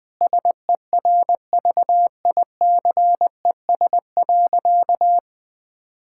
Morse code, heard as a keyed tone, about 20 wpm, 700 Hz. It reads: SERVICES.
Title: Sentences with 4-Word Phrases - 20wpm